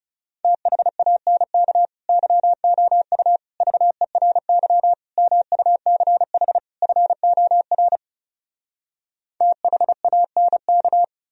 Morse code, heard as a keyed tone, about 40 wpm, 700 Hz. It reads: THANK YOU VERY MUCH FOR THANK